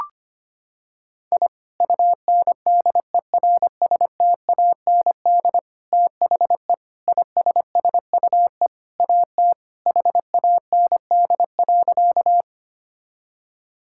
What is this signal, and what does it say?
Morse code, heard as a keyed tone, about 25 wpm, 700 Hz. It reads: I UNDERSTAND THE ISSUE AT HAND.
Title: Sentences from Top 1000 Words - Review Part 6 - 25wpm